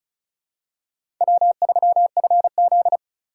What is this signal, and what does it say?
Morse code, heard as a keyed tone, about 36 wpm, 700 Hz. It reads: W3FZ
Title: USA Call Signs - 4 Characters - 2x Character Spacing - 35wpm